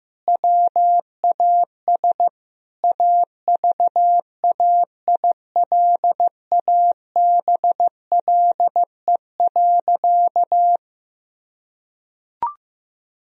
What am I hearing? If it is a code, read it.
Morse code, heard as a keyed tone, about 15 wpm, 700 Hz. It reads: WAS AVAILABLE.
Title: Sentences from Top 1000 Words - Review Part 3 - 15wpm